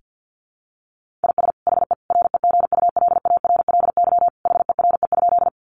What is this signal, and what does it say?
Morse code, heard as a keyed tone, about 50 wpm, 700 Hz. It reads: IS HE REGULARLY HERE?
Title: Sentences from Top 1400 Words 50wpm